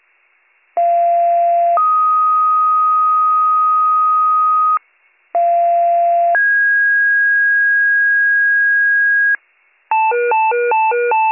Two-tone page, heard as a one-second tone, then a three-second tone, about 690 then 1250 Hz, about 690 then 1650 Hz, with noise. It is heard from 0.8 to 9.4 s.